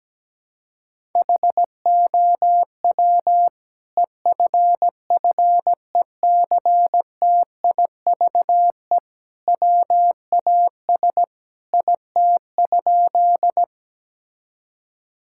Morse code, heard as a keyed tone, about 17 wpm, 700 Hz. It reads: HOW EFFECTIVE WAS IT?
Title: Sentences from Top 1000 Words - Review Part 6 - 17wpm